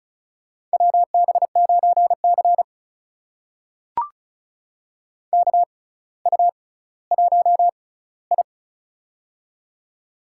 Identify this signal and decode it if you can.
Morse code, heard as a keyed tone, about 36 wpm, spaced as for 18 wpm, 700 Hz. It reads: WB9C KU1I
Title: USA Call Signs - 6x Character Spacing - 35wpm